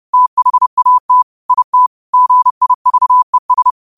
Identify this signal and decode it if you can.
Morse code, heard as a keyed tone, about 30 wpm, 1000 Hz. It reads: THAT IT GIVES
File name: qotdfiles / 30wpm